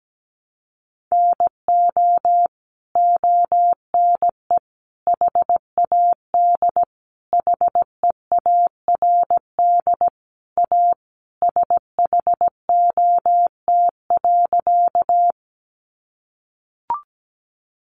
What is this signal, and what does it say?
Morse code, heard as a keyed tone, about 17 wpm, 700 Hz. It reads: NO ONE HAD HEARD A SHOT.
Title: Sentences from Top 1100 Words 17wpm